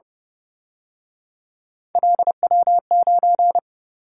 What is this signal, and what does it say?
Morse code, heard as a keyed tone, about 29 wpm, 700 Hz. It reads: LW9
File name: Speed-Racing - Three Letter-Number Combinations - 25wpm